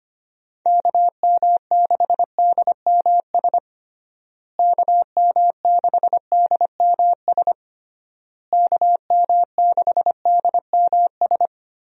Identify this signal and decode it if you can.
Morse code, heard as a keyed tone, about 25 wpm, 700 Hz. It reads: KM6DMS KM6DMS KM6DMS
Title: USA Call Signs - 6 Characters - Repeated 3x - 25wpm